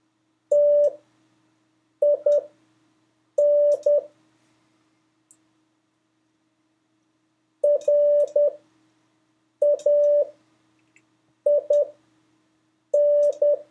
Morse code, heard as a keyed tone, about 10 wpm, spaced as for 6 wpm, 580 Hz. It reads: TIN RAIN
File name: 2of5-words-1